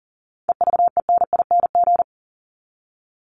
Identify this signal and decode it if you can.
Morse code, heard as a keyed tone, about 40 wpm, 700 Hz. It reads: EVENING
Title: Top 1501-1750 Words 40wpm